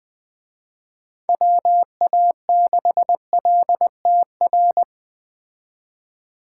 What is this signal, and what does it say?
Morse code, heard as a keyed tone, about 20 wpm, 700 Hz. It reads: WA6LTR